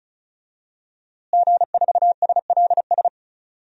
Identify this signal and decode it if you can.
Morse code, heard as a keyed tone, about 34 wpm, 700 Hz. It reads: G4SLS
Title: DXCC-100 Call Signs - 5 Characters - Repeated 3x - 35wpm